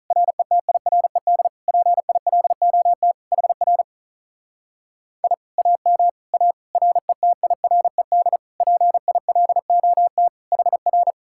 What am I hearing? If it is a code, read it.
Morse code, heard as a keyed tone, about 44 wpm, 700 Hz. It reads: RETIRED PILOT HR I AM A RETIRED PILOT HR
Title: Speed-Racing Common QSO Phrases Encore - 35wpm